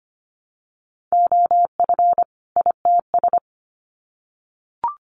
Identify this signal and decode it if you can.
Morse code, heard as a keyed tone, about 25 wpm, 700 Hz. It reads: OF ITS
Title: Sets of 2 Words 25wpm